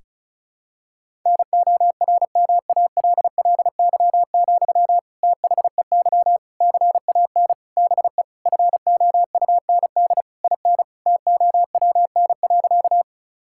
Morse code, heard as a keyed tone, about 36 wpm, 700 Hz. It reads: NORMALLY, THEY CAN BE FOUND IN TOWN.
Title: Sentences from Top 1000 Words - Review Part 6 - 35wpm